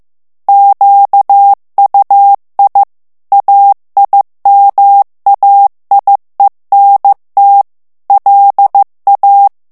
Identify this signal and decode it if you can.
Morse code, heard as a keyed tone, about 15 wpm, 800 Hz. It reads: QUI AIMAIENT LA
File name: morse mp3 / FRANCAIS TEXTE CLAIR 2 VITESSE 14 MOTS MINUTES